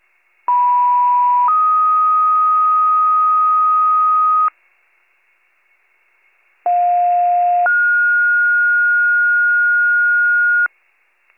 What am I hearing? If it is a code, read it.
Two-tone page, heard as a one-second tone, then a three-second tone, about 980 then 1250 Hz, about 710 then 1430 Hz. It is heard from 0.5 to 10.7 s.